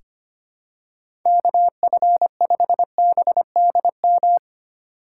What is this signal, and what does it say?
Morse code, heard as a keyed tone, about 25 wpm, 700 Hz. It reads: KF5BDM
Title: USA Call Signs - 6 Characters - 2x Character Spacing - 25wpm